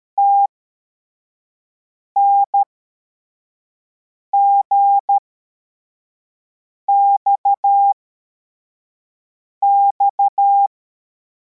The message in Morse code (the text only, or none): TNGXX